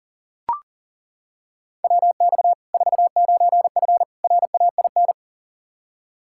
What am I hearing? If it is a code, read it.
Morse code, heard as a keyed tone, about 40 wpm, 700 Hz. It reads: WX 49F RAIN
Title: Practice QSO - Exchange 2 - Info - 40wpm